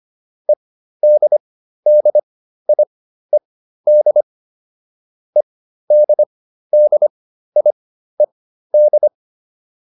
Morse code, heard as a keyed tone, about 25 wpm, spaced as for 13 wpm, 600 Hz. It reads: EDDIED EDDIED